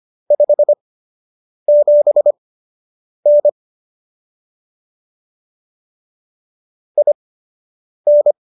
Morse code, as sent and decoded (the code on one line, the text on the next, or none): ..... --... -. / .. -.
57N IN